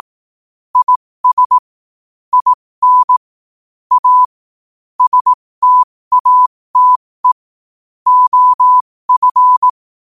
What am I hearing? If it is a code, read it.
Morse code, heard as a keyed tone, about 18 wpm, spaced as for 15 wpm, 1000 Hz. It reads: IS IN A STATE OF
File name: qotdfiles / 15wpm